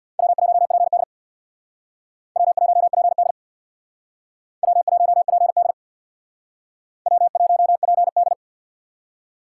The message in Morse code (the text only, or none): W1PD W1PD W1PD W1PD